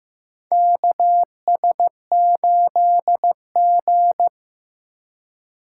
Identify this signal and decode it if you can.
Morse code, heard as a keyed tone, about 15 wpm, 700 Hz. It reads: KS8G
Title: USA Call Signs - 4x Character Spacing - 15wpm